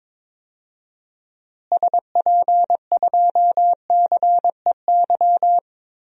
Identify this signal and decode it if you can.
Morse code, heard as a keyed tone, about 22 wpm, 700 Hz. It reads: SP2CEY
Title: DXCC-100 Call Signs - 6 Characters - Repeated 3x - 22wpm